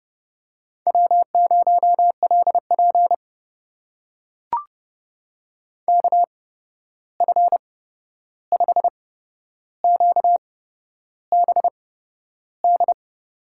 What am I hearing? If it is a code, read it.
Morse code, heard as a keyed tone, about 30 wpm, spaced as for 11 wpm, 700 Hz. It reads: W0LP KF5QBD
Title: USA Call Signs - 8x Character Spacing - 30wpm